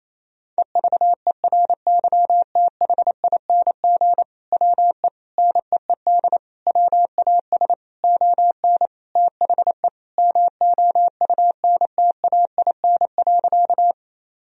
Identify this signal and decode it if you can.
Morse code, heard as a keyed tone, about 27 wpm, 700 Hz. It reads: EVERYTHING WE NEED WAS ON THE MOUNTAIN.
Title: Sentences from Top 1000 Words - Review Part 1 - 28wpm